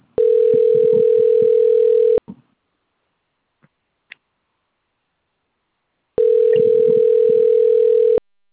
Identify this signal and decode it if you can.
Ringback tone (North America), 440 + 480 Hz.
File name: rg-1600-unknown- - - .85758